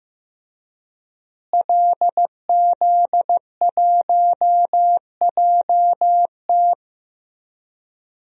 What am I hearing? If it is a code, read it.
Morse code, heard as a keyed tone, about 15 wpm, 700 Hz. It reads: LZ1JT
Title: DXCC-100 Call Signs - 5 Characters - Repeated 3x - 15wpm